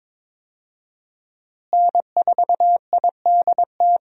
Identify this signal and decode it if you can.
Morse code, heard as a keyed tone, about 22 wpm, 700 Hz. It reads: N4IDT
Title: USA Call Signs - 5 Characters - Repeated 3x - 22wpm